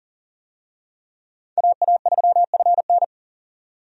Morse code, heard as a keyed tone, about 40 wpm, 700 Hz. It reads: AA3FN
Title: USA Call Signs - 5 Characters - 40wpm